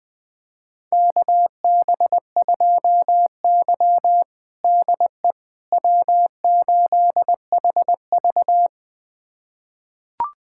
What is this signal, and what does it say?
Morse code, heard as a keyed tone, about 21 wpm, 700 Hz. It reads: KB2Y DE W8HV
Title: USA Callsign DE Callsign - 4 Characters - 20wpm